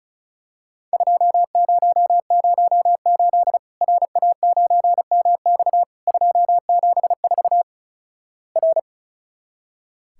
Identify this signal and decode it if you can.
Morse code, heard as a keyed tone, about 36 wpm, 700 Hz. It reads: 2008 RA9MX 274 R
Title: Worked All Europe DX Contest - QTC Exchange - 35wpm